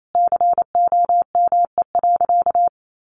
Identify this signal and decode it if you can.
Morse code, heard as a keyed tone, about 28 wpm, 700 Hz. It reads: COME.
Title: Sentences from Top 1300 Words 28wpm